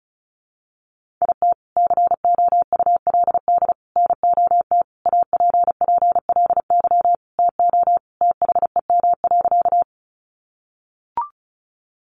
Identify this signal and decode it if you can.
Morse code, heard as a keyed tone, about 34 wpm, 700 Hz. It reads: IT COULD NOT APPLY TO THEM.
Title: Sentences from Top 1200 Words 35wpm